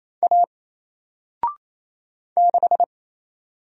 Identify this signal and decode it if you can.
Morse code, heard as a keyed tone, about 27 wpm, 700 Hz. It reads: A 6
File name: Single-Letter-Number-28wpm